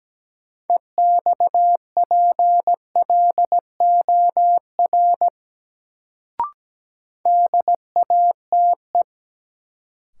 Morse code, heard as a keyed tone, about 17 wpm, 700 Hz. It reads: EXPLOR DATE
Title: Sentences from Top 1100 Words - New Word Drill Alt - 17wpm